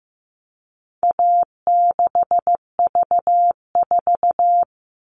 Morse code, heard as a keyed tone, about 15 wpm, 700 Hz. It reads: A6V4
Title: All Cut Numbers - 4-Digit 15wpm